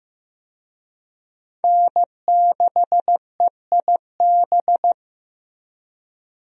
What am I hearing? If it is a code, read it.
Morse code, heard as a keyed tone, about 15 wpm, 700 Hz. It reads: N6EIB
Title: USA Call Signs - 5 Characters - Repeated 3x - 15wpm